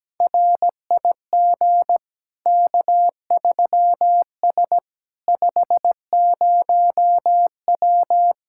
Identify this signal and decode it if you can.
Morse code, heard as a keyed tone, about 17 wpm, 700 Hz. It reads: RIG K3S 50W
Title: K1USN SST QSO Practice 17wpm